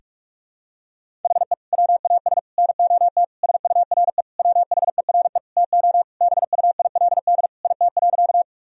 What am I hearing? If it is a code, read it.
Morse code, heard as a keyed tone, about 46 wpm, 700 Hz. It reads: HE WAS NOT SURE WHERE TO BUILD IT.
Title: Sentences from Top 1000 Words - Review Part 1 - 45wpm